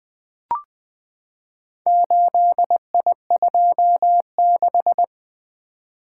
Morse code, heard as a keyed tone, about 20 wpm, 700 Hz. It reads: 8I26
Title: Four Letter-Number Combinations - 20wpm